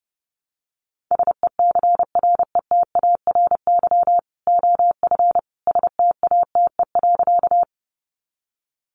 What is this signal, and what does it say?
Morse code, heard as a keyed tone, about 29 wpm, 700 Hz. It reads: SECRETARY OF STATE.
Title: Sentences from Top 1300 Words 30wpm